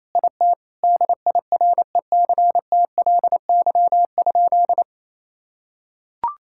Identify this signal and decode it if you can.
Morse code, heard as a keyed tone, about 28 wpm, 700 Hz. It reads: IT DIRECTLY?
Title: Sentences from Top 1000 Words - Review Part 5 - 28wpm